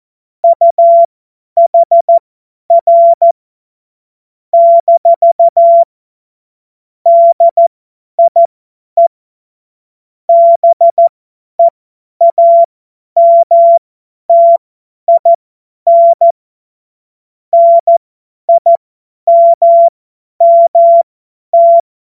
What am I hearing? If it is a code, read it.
Morse code, heard as a keyed tone, about 14 wpm, spaced as for 10 wpm, 680 Hz. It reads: UHR - DIE BEAMTIN NIMMT
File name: Witz-28-Oktober-2025-50-F70